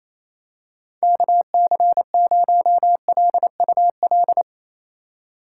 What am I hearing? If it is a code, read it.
Morse code, heard as a keyed tone, about 28 wpm, 700 Hz. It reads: KC0LUL
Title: USA Call Signs - 6 Characters - 3x Character Spacing - 28wpm